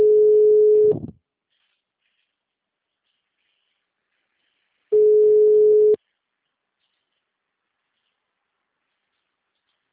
Ringback tone (Europe, 425 Hz).